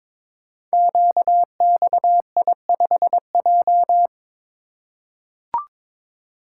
Morse code, heard as a keyed tone, about 22 wpm, 700 Hz. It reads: QXI5J
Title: Code Groups - 5 Letter:Numbers - 22wpm